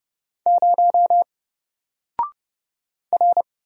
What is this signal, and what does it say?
Morse code, heard as a keyed tone, about 30 wpm, 700 Hz. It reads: 0 R
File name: Single-Letter-Number-30wpm